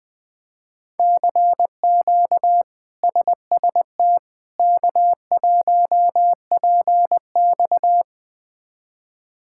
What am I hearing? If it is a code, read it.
Morse code, heard as a keyed tone, about 20 wpm, 700 Hz. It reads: CQ SST K1PX
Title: K1USN SST QSO Practice - 20wpm